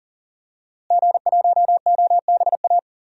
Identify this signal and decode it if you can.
Morse code, heard as a keyed tone, about 40 wpm, 700 Hz. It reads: G1OBA
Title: DXCC-100 Call Signs - 5 Characters - Repeated 3x - 40wpm